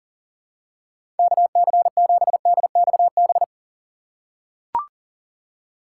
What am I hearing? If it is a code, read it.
Morse code, heard as a keyed tone, about 40 wpm, 700 Hz. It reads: KC7DXB
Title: USA Call Signs - 6 Characters - 4x Character Spacing - 40wpm